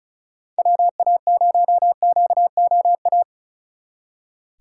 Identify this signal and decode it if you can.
Morse code, heard as a keyed tone, about 41 wpm, 700 Hz. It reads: WA0QOA
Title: Speed Racing - USA 2x3 Call Signs 35wpm